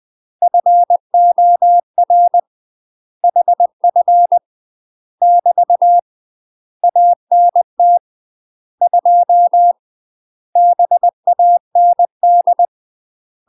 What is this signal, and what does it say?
Morse code, heard as a keyed tone, about 20 wpm, 700 Hz. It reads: FOR HF <BT> ANT 2 BAND
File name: UK MASTER QSOs TRAINING / 20WPM / UK MASTER QSOs 0059